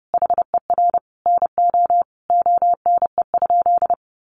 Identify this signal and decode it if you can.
Morse code, heard as a keyed tone, about 30 wpm, 700 Hz. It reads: HER NO ONE?
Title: Sentences from Top 600 Words 30wpm